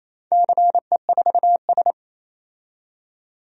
Morse code, heard as a keyed tone, about 28 wpm, 700 Hz. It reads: CE4S
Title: DXCC-100 Call Signs - 4 Characters - 28wpm